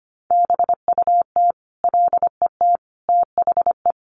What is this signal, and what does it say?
Morse code, heard as a keyed tone, about 25 wpm, 700 Hz. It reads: BUT LET THE
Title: Sentences from Top 1300 Words 25wpm